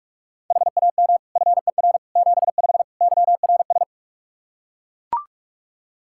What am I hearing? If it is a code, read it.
Morse code, heard as a keyed tone, about 41 wpm, 700 Hz. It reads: HAM FER 75 YRS
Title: K1USN SST QSO Practice 45wpm